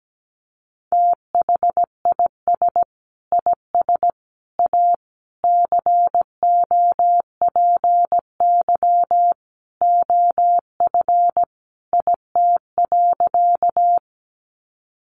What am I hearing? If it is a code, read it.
Morse code, heard as a keyed tone, about 17 wpm, 700 Hz. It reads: THIS IS A COPY OF IT.
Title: Sentences from Top 1400 Words 17wpm